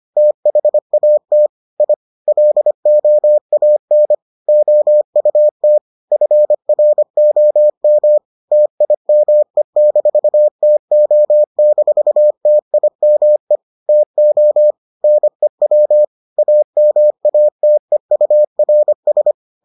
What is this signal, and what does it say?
Morse code, heard as a keyed tone, about 25 wpm, 600 Hz. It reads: THAT I LOAN OUT FROM TIME-TO-TIME TO NEW AMATEURS